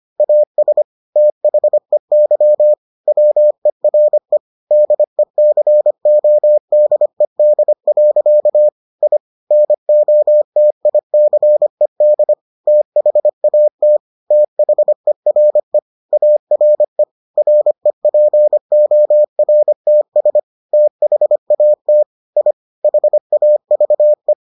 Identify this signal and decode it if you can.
Morse code, heard as a keyed tone, about 25 wpm, 600 Hz. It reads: AS THEY WERE DECODED. I NOTICED THAT THERE ARE REPORTS THAT I HAVE